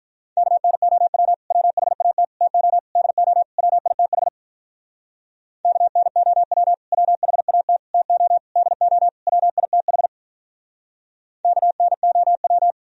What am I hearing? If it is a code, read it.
Morse code, heard as a keyed tone, about 45 wpm, 700 Hz. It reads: KNOW WHAT TO DO WITH KNOW WHAT TO DO WITH KNOW